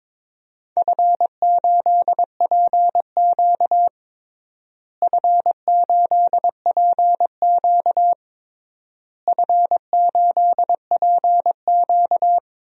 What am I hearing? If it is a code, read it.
Morse code, heard as a keyed tone, about 22 wpm, 700 Hz. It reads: F8PQ F8PQ F8PQ
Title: DXCC-100 Call Signs - 4 Characters - Repeated 3x - 22wpm